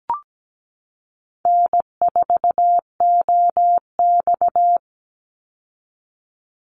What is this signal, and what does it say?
Morse code, heard as a keyed tone, about 17 wpm, 700 Hz. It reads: N4OX
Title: Easy Call Signs Repeated 3x - 17wpm